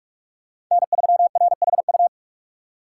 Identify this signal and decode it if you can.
Morse code, heard as a keyed tone, about 44 wpm, 700 Hz. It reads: N3RHU